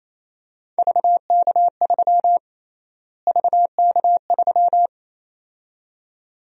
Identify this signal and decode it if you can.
Morse code, heard as a keyed tone, about 28 wpm, 700 Hz. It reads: VK3 VK3
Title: USA Call Sign Prefixes - Repeated 2x - 28wpm